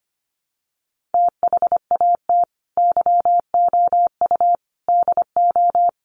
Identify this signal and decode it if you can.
Morse code, heard as a keyed tone, about 25 wpm, 700 Hz. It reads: THAT YOU DO
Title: Sets of 4 Words 25wpm